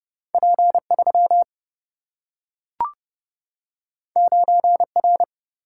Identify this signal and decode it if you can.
Morse code, heard as a keyed tone, about 30 wpm, 700 Hz. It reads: P 3 9 R
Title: Single-Letter Single-Number 30wpm